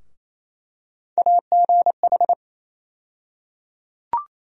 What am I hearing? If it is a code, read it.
Morse code, heard as a keyed tone, about 28 wpm, 700 Hz. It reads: AGH
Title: USA Call Sign Postfixes - 3 Letters 28wpm